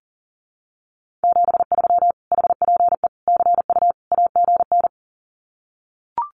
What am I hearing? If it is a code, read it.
Morse code, heard as a keyed tone, about 40 wpm, 700 Hz. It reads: 73 HPE CU AGN
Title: Common QSO Phrases Encore 40wpm